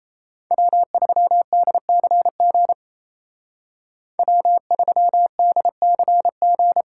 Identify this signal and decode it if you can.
Morse code, heard as a keyed tone, about 36 wpm, 700 Hz. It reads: W3DCG W3DCG